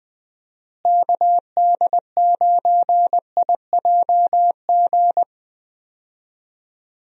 Morse code, heard as a keyed tone, about 20 wpm, 700 Hz. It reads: KD9IJG
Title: USA Call Signs - 6 Characters - Repeated 3x - 20wpm